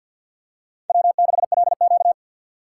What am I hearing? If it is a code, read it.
Morse code, heard as a keyed tone, about 52 wpm, 700 Hz. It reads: W6LQ
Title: USA Call Signs - 4 Characters - 3x Character Spacing - 50wpm